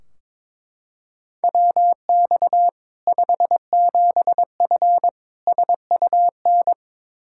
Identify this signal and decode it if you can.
Morse code, heard as a keyed tone, about 21 wpm, 700 Hz. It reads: WX 57F SUN
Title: K1USN SST QSO Practice 22wpm